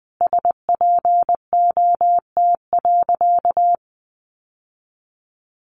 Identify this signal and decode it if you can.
Morse code, heard as a keyed tone, about 20 wpm, 700 Hz. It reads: SPOT.
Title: Sentences from Top 1200 Words 20wpm